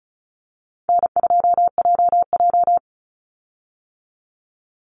Morse code, heard as a keyed tone, about 34 wpm, 700 Hz. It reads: N2JJ
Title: Call Signs - 35wpm